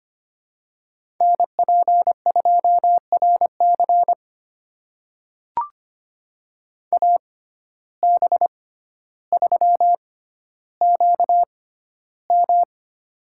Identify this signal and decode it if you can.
Morse code, heard as a keyed tone, about 25 wpm, spaced as for 13 wpm, 700 Hz. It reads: NP2RC AB3QM
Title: USA Call Signs - 5 Characters - 6x Character Spacing - 25wpm